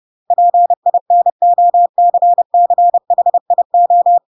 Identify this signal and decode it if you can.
Morse code, heard as a keyed tone, about 30 wpm, 700 Hz. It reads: PINOCCHIO
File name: Pinocchio / Pinocchio ch6 30 30 ik1hge